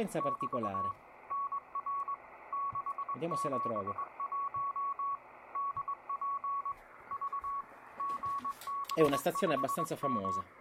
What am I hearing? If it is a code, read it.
Morse code, heard as a keyed tone, about 20 wpm, 1120 Hz, with speech and radio noise. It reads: <BT> NR 671 DP V GB5M